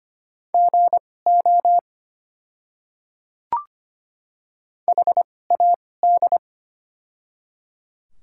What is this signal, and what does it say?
Morse code, heard as a keyed tone, about 25 wpm, spaced as for 18 wpm, 690 Hz. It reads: GO HAD
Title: 2x Character Spacing - Top 100 Words - 25wpm